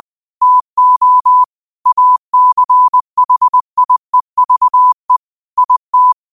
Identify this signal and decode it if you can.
Morse code, heard as a keyed tone, about 20 wpm, 1000 Hz. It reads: TO ACHIEVE IT